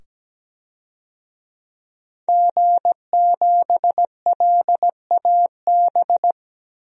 Morse code, heard as a keyed tone, about 17 wpm, 700 Hz. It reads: G7LAB
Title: DXCC-100 Call Signs - 5 Characters - 17wpm